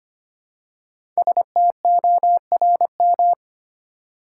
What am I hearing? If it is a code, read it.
Morse code, heard as a keyed tone, about 25 wpm, 700 Hz. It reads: STORM
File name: 5 Letter Words - First 3 Letters Given - 25wpm